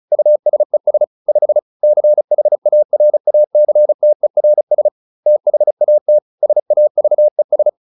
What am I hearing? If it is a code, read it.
Morse code, heard as a keyed tone, about 34 wpm, 600 Hz. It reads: USES 5 CHARACTERS THAT SAVES